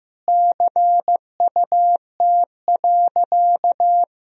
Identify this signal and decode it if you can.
Morse code, heard as a keyed tone, about 15 wpm, 700 Hz. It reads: CUT.
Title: Sentences from Top 1000 Words - Review Part 1 - 15wpm